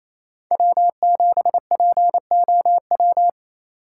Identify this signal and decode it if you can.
Morse code, heard as a keyed tone, about 28 wpm, 700 Hz. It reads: W7POW
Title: USA Call Signs - 4x Character Spacing - 28wpm